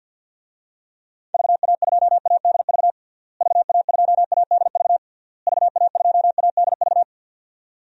Morse code, heard as a keyed tone, about 50 wpm, 700 Hz. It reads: VA2ADV VA2ADV VA2ADV
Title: DXCC-100 Call Signs - 6 Characters - Repeated 3x - 50wpm